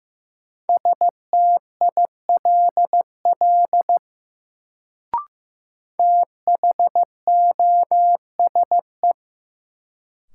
Morse code, heard as a keyed tone, about 15 wpm, 700 Hz. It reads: STILL THOSE
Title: Sentences from Top 300 Words - New Word Drill - 15wpm